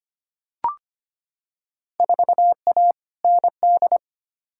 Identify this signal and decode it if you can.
Morse code, heard as a keyed tone, about 25 wpm, 700 Hz. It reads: 4A ND
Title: Field Day Exchanges Encore - 25wpm